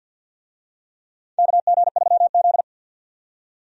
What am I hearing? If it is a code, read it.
Morse code, heard as a keyed tone, about 51 wpm, 700 Hz. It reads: KG3Z
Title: USA Call Signs - 4 Characters - 50wpm